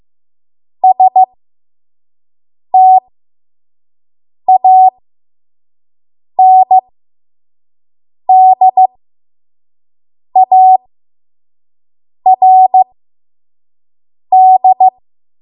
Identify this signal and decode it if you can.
Morse code, heard as a keyed tone, about 15 wpm, spaced as for 5 wpm, 750 Hz. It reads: STANDARD
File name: Morse / 05WPM